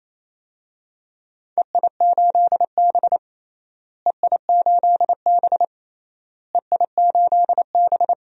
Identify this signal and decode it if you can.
Morse code, heard as a keyed tone, about 28 wpm, 700 Hz. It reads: EI8B EI8B EI8B
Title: DXCC-100 Call Signs - 4 Characters - Repeated 3x - 28wpm